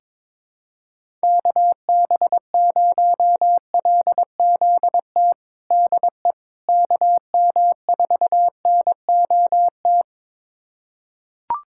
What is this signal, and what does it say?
Morse code, heard as a keyed tone, about 22 wpm, 700 Hz. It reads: KB0LZT DE KM4NOT
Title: USA Callsign DE Callsign - 6 Characters - 22wpm